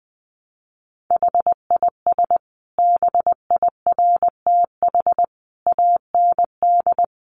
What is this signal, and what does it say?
Morse code, heard as a keyed tone, about 20 wpm, 700 Hz. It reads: HIS BIRTH AND
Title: Sentences from Top 900 Words 20wpm